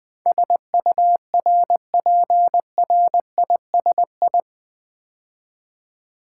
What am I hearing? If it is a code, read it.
Morse code, heard as a keyed tone, about 20 wpm, 700 Hz. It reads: SURPRISI
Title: Build A Word - 4 Letters At A Time - 10-Letter Words - 20wpm